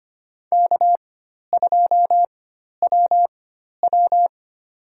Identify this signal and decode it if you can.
Morse code, heard as a keyed tone, about 25 wpm, spaced as for 12 wpm, 700 Hz. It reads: K2WW
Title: USA Call Signs - 4x Character Spacing - 25wpm